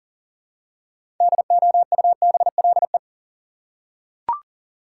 Morse code, heard as a keyed tone, about 40 wpm, 700 Hz. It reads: DOUBLE
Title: Six Letter Words - 40wpm